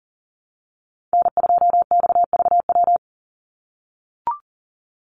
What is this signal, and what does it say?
Morse code, heard as a keyed tone, about 40 wpm, 700 Hz. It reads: N2XVW
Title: USA Call Signs 40wpm